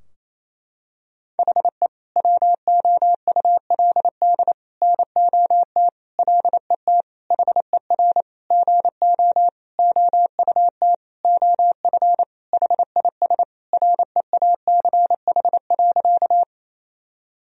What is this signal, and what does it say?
Morse code, heard as a keyed tone, about 28 wpm, 700 Hz. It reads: HE WOULD NOT LET HER GO OUT OF HIS REACH.
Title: Sentences from Top 1000 Words - Review Part 2 - 28wpm